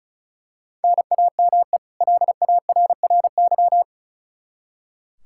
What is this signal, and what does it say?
Morse code, heard as a keyed tone, about 34 wpm, 700 Hz. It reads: NAME LARRY
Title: Practice QSO - Exchange 1 - Beginning - 35wpm